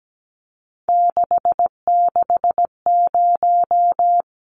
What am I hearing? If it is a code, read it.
Morse code, heard as a keyed tone, about 17 wpm, 700 Hz. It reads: 660
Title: Numbers - 3-Digit 17wpm